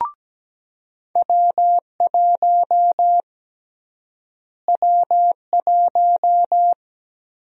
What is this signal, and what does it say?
Morse code, heard as a keyed tone, about 17 wpm, 700 Hz. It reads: W1 W1
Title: USA Call Sign Prefixes - Repeated 2x - 17wpm